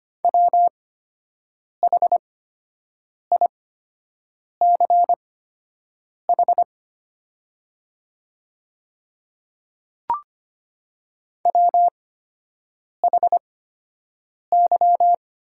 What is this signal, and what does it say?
Morse code, heard as a keyed tone, about 25 wpm, spaced as for 7 wpm, 690 Hz. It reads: WHICH WHY
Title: 8x Character Spacing - Top 500 Words - 25wpm